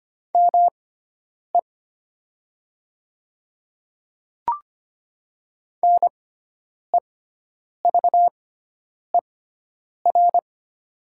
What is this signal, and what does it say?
Morse code, heard as a keyed tone, about 25 wpm, spaced as for 9 wpm, 700 Hz. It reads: ME NEVER
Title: 6x Character Spacing - Top 300 Words - 25wpm